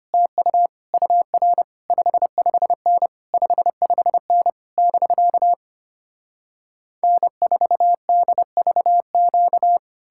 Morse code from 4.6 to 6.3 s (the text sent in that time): <BK>